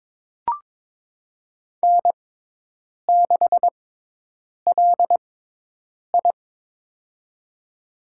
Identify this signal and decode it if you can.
Morse code, heard as a keyed tone, about 22 wpm, spaced as for 8 wpm, 700 Hz. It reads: N6LI
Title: USA Call Signs - 4 Characters - 6x Character Spacing - 22wpm